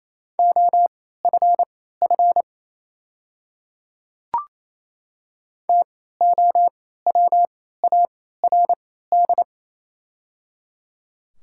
Morse code, heard as a keyed tone, about 27 wpm, spaced as for 16 wpm, 700 Hz. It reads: OFF TOWARD
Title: 3x Character Spacing - Top 500 Words - 28wpm